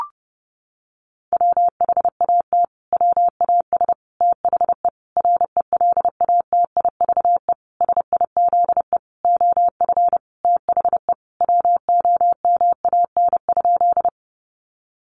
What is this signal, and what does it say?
Morse code, heard as a keyed tone, about 28 wpm, 700 Hz. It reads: WHAT WAS THE RELATIVE SIZE OF THE WOMAN?
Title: Sentences from Top 1400 Words 30wpm